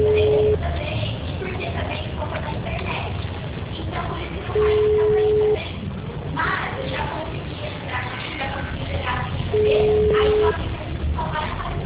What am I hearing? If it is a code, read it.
Ringback tone (Europe, 425 Hz), with speech.
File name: out- -2014- - - .8575